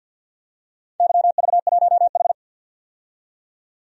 Morse code, heard as a keyed tone, about 47 wpm, 700 Hz. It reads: YV1H
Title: DXCC-100 Call Signs - 4 Characters - 50wpm